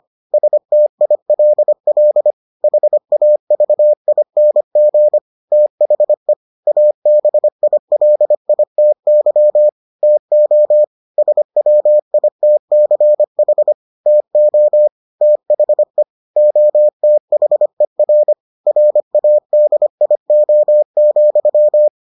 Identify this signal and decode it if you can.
Morse code, heard as a keyed tone, about 25 wpm, 600 Hz. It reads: STILL HAVING THE ABILITY TO SWITCH TO THE OTHER RADIO,